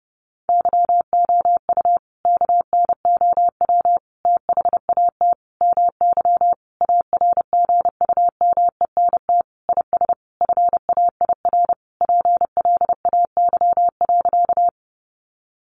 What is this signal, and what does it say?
Morse code, heard as a keyed tone, about 30 wpm, 700 Hz. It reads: YOU KNOW THAT MY ARGUMENT IS FAIR PLAY.
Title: Sentences from Top 1400 Words 30wpm